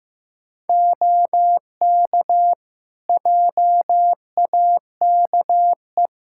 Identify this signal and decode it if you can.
Morse code, heard as a keyed tone, about 15 wpm, 700 Hz. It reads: OK JAKE
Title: Practice QSO - Exchange 3 - More Info - 15wpm